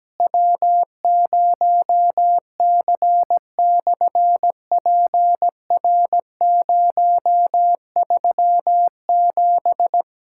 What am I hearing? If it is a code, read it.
Morse code, heard as a keyed tone, about 17 wpm, 700 Hz. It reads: W0C/PR037
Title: SOTA QSO Practice 17wpm